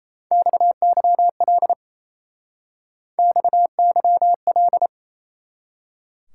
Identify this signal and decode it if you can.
Morse code, heard as a keyed tone, about 34 wpm, 700 Hz. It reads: XYL XYL
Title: Speed-Racing Top 180 QSO Elements 28wpm